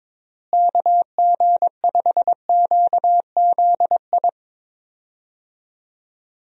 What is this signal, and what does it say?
Morse code, heard as a keyed tone, about 21 wpm, 700 Hz. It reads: KG5QZI
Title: USA Call Signs - 6 Characters - Repeated 3x - 22wpm